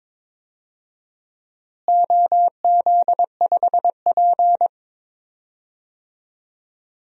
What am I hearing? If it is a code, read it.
Morse code, heard as a keyed tone, about 22 wpm, 700 Hz. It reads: OZ5P
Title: DXCC-100 Call Signs - 4 Characters - Repeated 2x - 22wpm